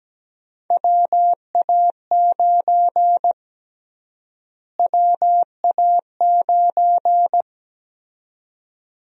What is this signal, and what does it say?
Morse code, heard as a keyed tone, about 17 wpm, 700 Hz. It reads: WA9 WA9
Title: USA Call Sign Prefixes - Repeated 2x - 17wpm